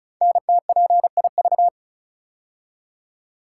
Morse code, heard as a keyed tone, about 36 wpm, 700 Hz. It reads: NTPIV
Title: Code Groups - 5 Letter:Numbers - 35wpm